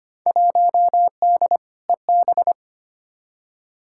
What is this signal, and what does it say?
Morse code, heard as a keyed tone, about 25 wpm, 700 Hz. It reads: 1D EB
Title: Field Day Exchanges - 25wpm